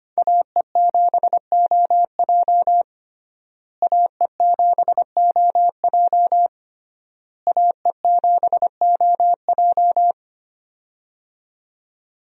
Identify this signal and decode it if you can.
Morse code, heard as a keyed tone, about 25 wpm, 700 Hz. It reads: AE7OJ AE7OJ AE7OJ
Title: USA Call Signs - 5 Characters - Repeated 3x - 25wpm